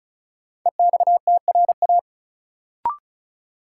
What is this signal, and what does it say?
Morse code, heard as a keyed tone, about 35 wpm, 700 Hz. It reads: EXTRA
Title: Sentences from Top 1000 Words - New Word Drill - 35wpm